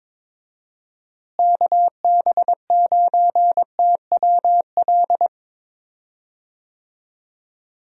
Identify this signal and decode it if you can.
Morse code, heard as a keyed tone, about 22 wpm, 700 Hz. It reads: KB9TWL
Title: USA Call Signs - 6 Characters - Repeated 2x - 22wpm